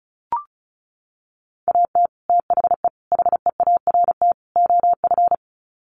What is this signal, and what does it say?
Morse code, heard as a keyed tone, about 34 wpm, 700 Hz. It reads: AT THE HEART OF
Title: Sets of 4 Words 35wpm